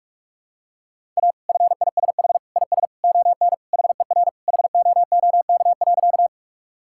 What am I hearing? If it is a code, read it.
Morse code, heard as a keyed tone, about 46 wpm, 700 Hz. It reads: A FISH IS ON HER HOOK.
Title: Sentences from Top 1600 Words 45wpm